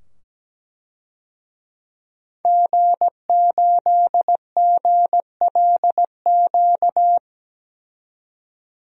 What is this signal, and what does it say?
Morse code, heard as a keyed tone, about 17 wpm, 700 Hz. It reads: G8GLQ
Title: DXCC-100 Call Signs - 5 Characters - 17wpm